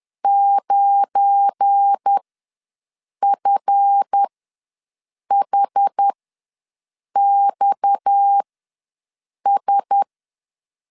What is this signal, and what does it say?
Morse code, heard as a keyed tone, about 10 wpm, spaced as for 6 wpm, 800 Hz. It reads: 9FHXS
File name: LEZIONI / 10 SIGNS / 10A SLASH ws 10wpm 4121